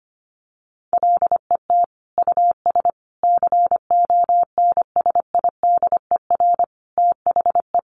Morse code, heard as a keyed tone, about 25 wpm, 700 Hz. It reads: LET US CONSIDER THE
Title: Sentences from Top 1300 Words 25wpm